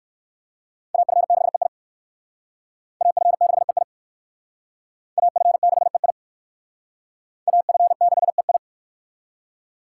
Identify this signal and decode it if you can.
Morse code, heard as a keyed tone, about 59 wpm, 700 Hz. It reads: AF6EI AF6EI AF6EI AF6EI